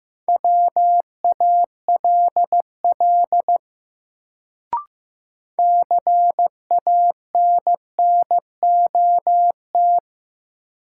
Morse code, heard as a keyed tone, about 15 wpm, 700 Hz. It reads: WALL CANNOT
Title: Sentences from Top 1000 Words - Review Part 6 - New Word Drill - 15wpm